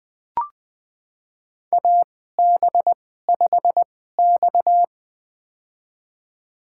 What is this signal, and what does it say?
Morse code, heard as a keyed tone, about 20 wpm, spaced as for 14 wpm, 700 Hz. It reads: AB5X
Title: USA Call Signs - 4 Characters - 2x Character Spacing - 20wpm